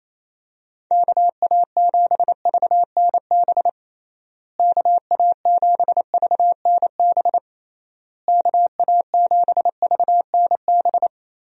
Morse code, heard as a keyed tone, about 28 wpm, 700 Hz. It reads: KA7VNB KA7VNB KA7VNB
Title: USA Call Signs - 6 Characters - Repeated 3x - 28wpm